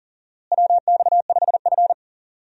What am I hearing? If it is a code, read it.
Morse code, heard as a keyed tone, about 40 wpm, 700 Hz. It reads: WX5F
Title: USA Call Signs - 4 Characters - Repeated 3x - 40wpm